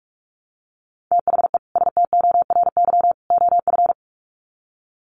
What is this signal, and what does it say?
Morse code, heard as a keyed tone, about 44 wpm, 700 Hz. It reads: THE STORY OF
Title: Sets of 3 Words 45wpm